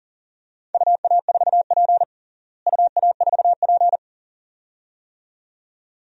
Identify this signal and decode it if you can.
Morse code, heard as a keyed tone, about 37 wpm, 700 Hz. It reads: UA4P UA4P
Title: DXCC-100 Call Signs - 4 Characters - Repeated 2x - 40wpm